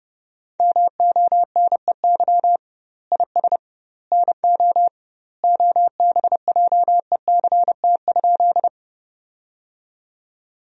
Morse code, heard as a keyed tone, about 30 wpm, 700 Hz. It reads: MONEY IS NO OBJECT?
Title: 2x Word-Spacing - Sentences from Top 500 Words - 30wpm